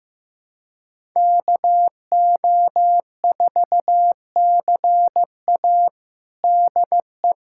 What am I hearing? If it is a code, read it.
Morse code, heard as a keyed tone, about 15 wpm, 700 Hz. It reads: KO4CA DE
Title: USA Callsign DE Callsign - 5 Characters - 15wpm